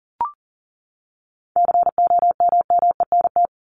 Morse code, heard as a keyed tone, about 40 wpm, 700 Hz. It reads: COMMENT
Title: Top 1751-2000 Words 40wpm